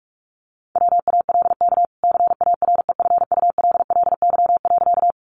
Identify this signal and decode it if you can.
Morse code, heard as a keyed tone, about 44 wpm, 700 Hz. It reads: WALK CAREFULLY.
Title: Sentences from Top 1300 Words 45wpm